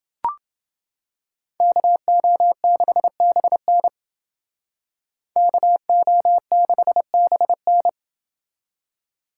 Morse code, heard as a keyed tone, about 28 wpm, 700 Hz. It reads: KO6BN KO6BN